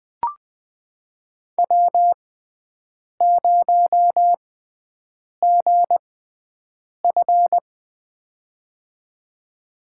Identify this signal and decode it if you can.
Morse code, heard as a keyed tone, about 20 wpm, spaced as for 7 wpm, 700 Hz. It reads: W0GF
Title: USA Call Signs - 4 Characters - 6x Character Spacing - 20wpm